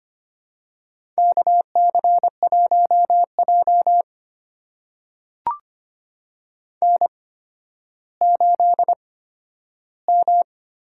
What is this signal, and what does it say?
Morse code, heard as a keyed tone, about 25 wpm, spaced as for 12 wpm, 700 Hz. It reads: KC1J N8M
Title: USA Call Signs - 8x Character Spacing - 25wpm